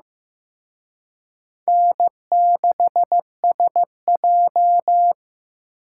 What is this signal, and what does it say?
Morse code, heard as a keyed tone, about 18 wpm, 700 Hz. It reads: N6SJ
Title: Speed Racing - USA 1x2 Call Signs 15wpm